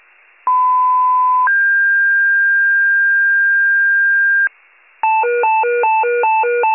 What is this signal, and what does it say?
Two-tone page, heard as a one-second tone, then a three-second tone, about 1010 then 1600 Hz. It is heard from 0.5 to 4.5 s.
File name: pager / 94 RESCUE DUTY / wcalert